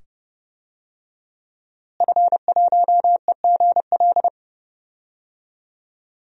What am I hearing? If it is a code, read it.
Morse code, heard as a keyed tone, about 30 wpm, 700 Hz. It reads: F1EGL
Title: DXCC-100 Call Signs - 5 Characters - Repeated 2x - 30wpm